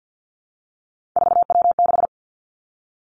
Morse code, heard as a keyed tone, about 50 wpm, 700 Hz. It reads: 4W6
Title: Three Challenging Letter-Number Combinations 50wpm